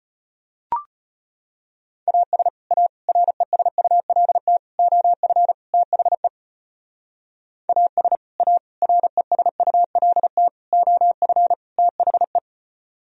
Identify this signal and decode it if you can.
Morse code, heard as a keyed tone, about 30 wpm, 700 Hz. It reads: AS A RESULT OF THE AS A RESULT OF THE
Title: Speed-Racing - Sets of 5 Words 25wpm